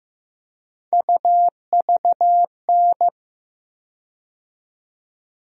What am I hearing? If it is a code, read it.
Morse code, heard as a keyed tone, about 15 wpm, 700 Hz. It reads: UVN
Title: USA Call Sign Postfixes - 3 Letters 15wpm